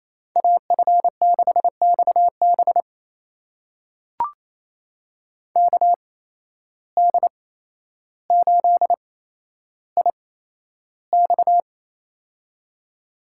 Morse code, heard as a keyed tone, about 28 wpm, spaced as for 12 wpm, 700 Hz. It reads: AF6XB KD8IX
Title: USA Call Signs - 5 Characters - 8x Character Spacing - 28wpm